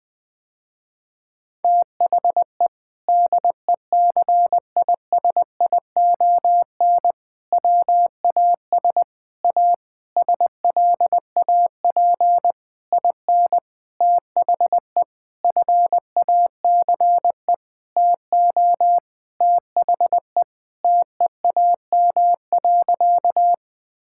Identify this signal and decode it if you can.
Morse code, heard as a keyed tone, about 20 wpm, 700 Hz. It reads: THE DECISION WAS A SLAP IN THE FACE TO THE TEAM.
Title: Sentences with 6-Word Phrases - Spoken Beginning - 20wpm